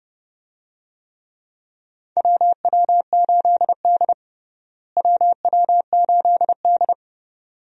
Morse code, heard as a keyed tone, about 30 wpm, 700 Hz. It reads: WW8D WW8D
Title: USA Call Signs - 4 Characters - Repeated 2x - 30wpm